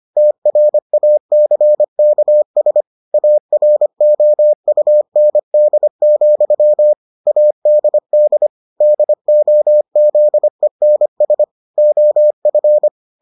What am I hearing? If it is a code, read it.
Morse code, heard as a keyed tone, about 25 wpm, 600 Hz. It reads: TRACKS AROUND, ADD DOZENS OF